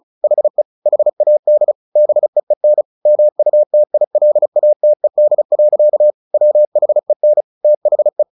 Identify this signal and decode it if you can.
Morse code, heard as a keyed tone, about 34 wpm, 600 Hz. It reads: HE HAD BEEN MUTILATED. WHEN THE
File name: the old man and the sea 2 35wpm